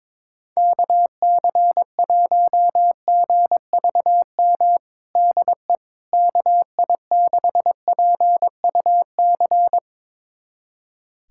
Morse code, heard as a keyed tone, about 22 wpm, 700 Hz. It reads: KC1GVM DE KI6PUC
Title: USA Callsign DE Callsign 22wpm